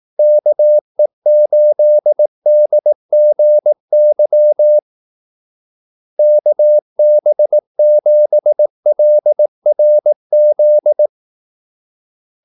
Morse code, heard as a keyed tone, about 18 wpm, 600 Hz. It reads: KE8DGY KB7LRZ
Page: Kaldesignaler 18 wpm